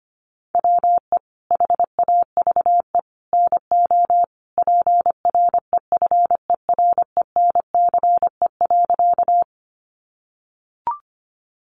Morse code, heard as a keyed tone, about 25 wpm, 700 Hz. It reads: WE HAVE NO PREFERENCE.
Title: Sentences from Top 1500 Words 25wpm